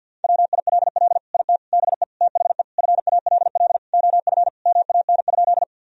Morse code, heard as a keyed tone, about 50 wpm, 700 Hz. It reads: WILL IT BE THE FALL OF MAN?
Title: Sentences from Top 1000 Words - Review Part 2 - 50wpm